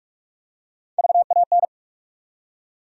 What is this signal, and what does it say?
Morse code, heard as a keyed tone, about 46 wpm, 700 Hz. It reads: VAN
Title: USA Call Sign Postfixes - 3 Letters 45wpm